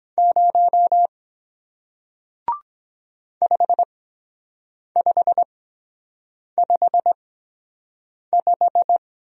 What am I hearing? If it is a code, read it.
Morse code, heard as a keyed tone, about 23 wpm, spaced as for 7 wpm, 700 Hz. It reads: 0 5555